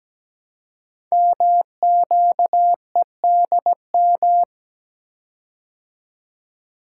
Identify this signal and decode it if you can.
Morse code, heard as a keyed tone, about 17 wpm, 700 Hz. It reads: MQEDM
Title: Code Groups - 5 Letter:Numbers - 17wpm